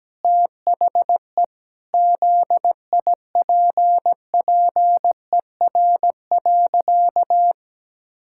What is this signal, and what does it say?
Morse code, heard as a keyed tone, about 17 wpm, 700 Hz. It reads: THE ZIPPER.
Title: Pangram Challenge v2 - 17wpm